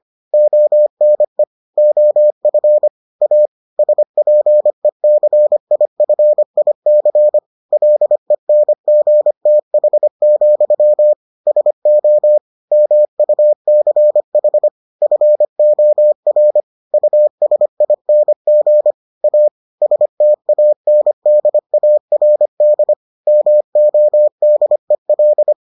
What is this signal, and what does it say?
Morse code, heard as a keyed tone, about 25 wpm, 600 Hz. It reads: ONE OF A SPECIFIC LENGTH, SO MUCH FOR USING A STANDARD MODEL